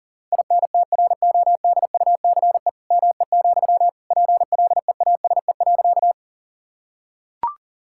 Morse code, heard as a keyed tone, about 40 wpm, 700 Hz. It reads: INTRODUCE ME, PLEASE.
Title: Sentences from Top 1600 Words 40wpm